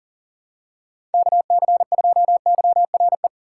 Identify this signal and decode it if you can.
Morse code, heard as a keyed tone, about 40 wpm, 700 Hz. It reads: KC2YRE
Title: USA Call Signs - 6 Characters - 3x Character Spacing - 40wpm